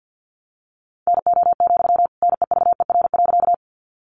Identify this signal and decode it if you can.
Morse code, heard as a keyed tone, about 48 wpm, 700 Hz. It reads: NO, NEVER.
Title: Sentences from Top 300 Words 50wpm